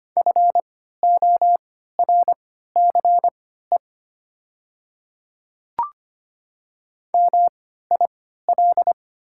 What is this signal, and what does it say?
Morse code, heard as a keyed tone, about 25 wpm, spaced as for 14 wpm, 700 Hz. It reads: FORCE MIL
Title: 3x Character Spacing - Top 500 Words - 25wpm